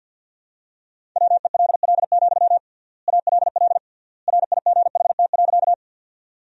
Morse code, heard as a keyed tone, about 52 wpm, 700 Hz. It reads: WELL, ALL RIGHT.
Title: 3x Word-Spacing - Sentences from Top 200 Words - 50wpm